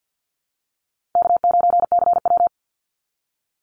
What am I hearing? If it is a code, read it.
Morse code, heard as a keyed tone, about 50 wpm, 700 Hz. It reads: K9CW
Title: Call Signs Repeated 2x - 50wpm